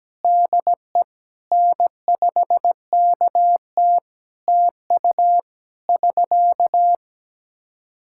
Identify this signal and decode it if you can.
Morse code, heard as a keyed tone, about 17 wpm, 700 Hz. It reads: DE N5KT TU <SK>
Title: Practice QSO - Exchange 4 - Ending - 17wpm